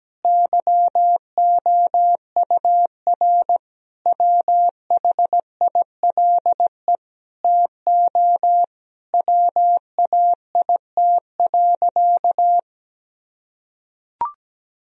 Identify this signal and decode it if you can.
Morse code, heard as a keyed tone, about 17 wpm, 700 Hz. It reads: YOUR WHILE TO WAIT.
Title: Sentences from Top 1000 Words - Review Part 1 - 17wpm